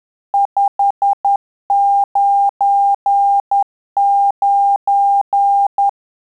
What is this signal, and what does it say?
Morse code, heard as a keyed tone, about 10 wpm, 800 Hz. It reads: 599
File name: Telegraphy-group / QSO CW 50 CPM